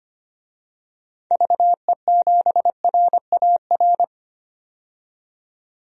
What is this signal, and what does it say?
Morse code, heard as a keyed tone, about 25 wpm, 700 Hz. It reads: VE7RAR
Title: DXCC-100 Call Signs - 6 Characters - 25wpm